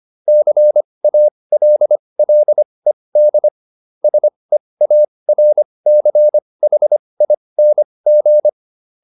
Morse code, heard as a keyed tone, about 25 wpm, spaced as for 20 wpm, 600 Hz. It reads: CALLED SEARCHING